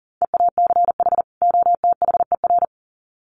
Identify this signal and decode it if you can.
Morse code, heard as a keyed tone, about 40 wpm, 700 Hz. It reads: EACH OTHER
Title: Sets of 2 Words 40wpm